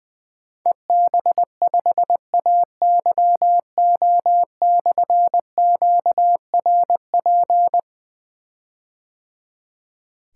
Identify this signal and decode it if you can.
Morse code, heard as a keyed tone, about 21 wpm, 700 Hz. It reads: EB5AYO/QRP
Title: Difficult Call Signs 20wpm